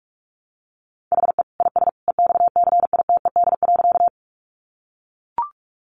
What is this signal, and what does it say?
Morse code, heard as a keyed tone, about 44 wpm, 700 Hz. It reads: HE IS EXCITED.
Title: Additional Easy Phrases 45wpm